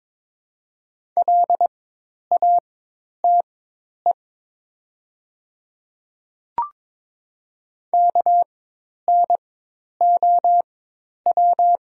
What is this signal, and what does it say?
Morse code, heard as a keyed tone, about 22 wpm, spaced as for 10 wpm, 700 Hz. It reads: LATE KNOW
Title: 4x Character Spacing - Top 300 Words - 22wpm